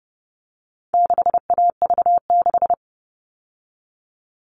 Morse code, heard as a keyed tone, about 30 wpm, 700 Hz. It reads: 6AV6
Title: All Cut Numbers - 4-Digit 30wpm